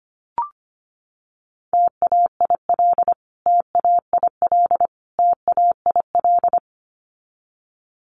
Morse code, heard as a keyed tone, about 25 wpm, 700 Hz. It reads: TAIL TAIL TAIL
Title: Top 500 Words Repeated 25wpm